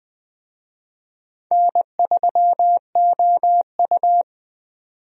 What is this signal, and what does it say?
Morse code, heard as a keyed tone, about 20 wpm, 700 Hz. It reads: N3OU
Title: USA Call Signs - 4 Characters - 20wpm